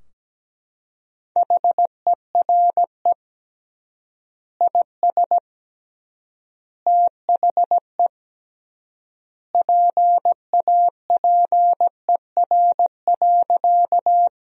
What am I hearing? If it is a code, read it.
Morse code, heard as a keyed tone, about 17 wpm, 700 Hz. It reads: HERE IS THE PAPER.
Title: 3x Word-Spacing - Sentences from Top 300 Words - 17wpm